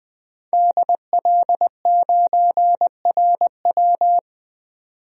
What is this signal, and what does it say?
Morse code, heard as a keyed tone, about 20 wpm, 700 Hz. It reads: DL9RW
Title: DXCC-100 Call Signs - 5 Characters - 20wpm